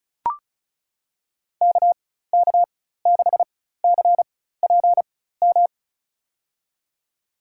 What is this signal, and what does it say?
Morse code, heard as a keyed tone, about 36 wpm, spaced as for 16 wpm, 700 Hz. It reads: KK6CPM
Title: USA Call Signs - 6 Characters - 4x Character Spacing - 35wpm